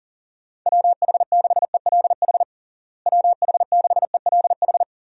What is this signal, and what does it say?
Morse code, heard as a keyed tone, about 37 wpm, 700 Hz. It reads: WH6ELH WH6ELH
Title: USA Call Signs - 6 Characters - Repeated 2x - 40wpm